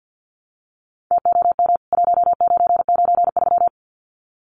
Morse code, heard as a keyed tone, about 49 wpm, 700 Hz. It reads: TOM 1993
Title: CWT Exchange 50wpm